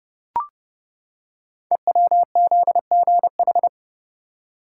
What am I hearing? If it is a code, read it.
Morse code, heard as a keyed tone, about 29 wpm, 700 Hz. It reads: EWZGH
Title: Code Groups - 5 Letters - 30wpm